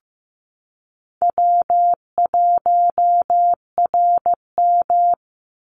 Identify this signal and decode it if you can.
Morse code, heard as a keyed tone, about 15 wpm, 700 Hz. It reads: W1RM
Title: Easy Call Signs Repeated 3x - 15wpm